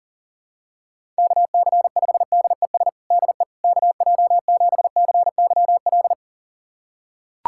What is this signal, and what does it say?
Morse code, heard as a keyed tone, about 40 wpm, 700 Hz. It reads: KC5DES DE KJ7CYL
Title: USA Callsign DE Callsign - 6 Characters - 40wpm